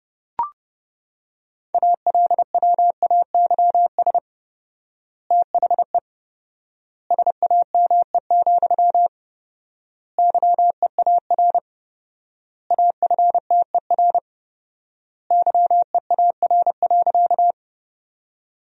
Morse code, heard as a keyed tone, about 30 wpm, 700 Hz. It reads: ALWAYS THE SAME, YEAR AFTER YEAR.
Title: 4x Word-Spacing - Sentences from Top 300 Words - 30wpm